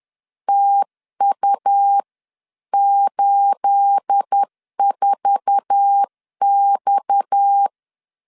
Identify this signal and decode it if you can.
Morse code, heard as a keyed tone, about 10 wpm, 800 Hz. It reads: TU84X
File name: LEZIONI / 9 28 10wpm 2842